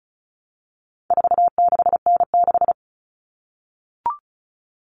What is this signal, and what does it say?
Morse code, heard as a keyed tone, about 34 wpm, 700 Hz. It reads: 46N6
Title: Common Cut Numbers - 4-Digit 35wpm